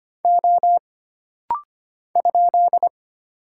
Morse code, heard as a keyed tone, about 25 wpm, 700 Hz. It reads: O ?
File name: Single Character (Letter:Number:Punctuation:Prosign) - Rapid-Fire - 25wpm